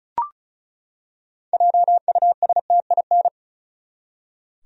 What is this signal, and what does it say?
Morse code, heard as a keyed tone, about 34 wpm, 700 Hz. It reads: JUSTIN
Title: USA Common Names - 35wpm